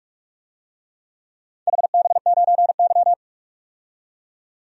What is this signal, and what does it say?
Morse code, heard as a keyed tone, about 46 wpm, 700 Hz. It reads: HB9Y
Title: DXCC-100 Call Signs - 4 Characters - 45wpm